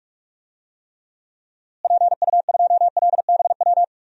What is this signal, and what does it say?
Morse code, heard as a keyed tone, about 46 wpm, 700 Hz. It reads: PU2LBW
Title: DXCC-100 Call Signs - 6 Characters - Repeated 3x - 45wpm